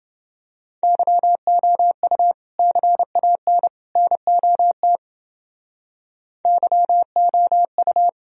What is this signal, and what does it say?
Morse code, heard as a keyed tone, about 30 wpm, 700 Hz. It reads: YOU CAN NOT YOU